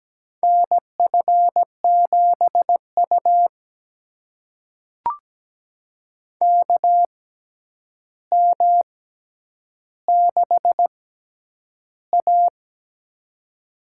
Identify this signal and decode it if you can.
Morse code, heard as a keyed tone, about 17 wpm, spaced as for 9 wpm, 700 Hz. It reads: NF7U KM6A
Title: USA Call Signs - 4 Characters - 6x Character Spacing - 17wpm